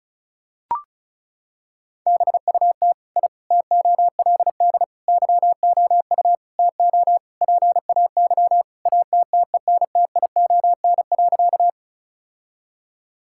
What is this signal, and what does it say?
Morse code, heard as a keyed tone, about 36 wpm, 700 Hz. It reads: BUT I TOLD YOU TO PAY ATTENTION.
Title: Sentences from Top 1000 Words - Review Part 6 - 35wpm